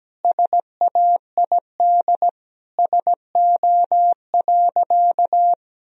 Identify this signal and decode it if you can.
Morse code, heard as a keyed tone, about 17 wpm, 700 Hz. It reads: SAID SO.
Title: Sentences from Top 1000 Words - Review Part 4 - 17wpm